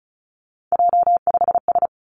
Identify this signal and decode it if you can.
Morse code, heard as a keyed tone, about 36 wpm, 700 Hz. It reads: J5S
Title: Three Challenging Letter-Number Combinations 35wpm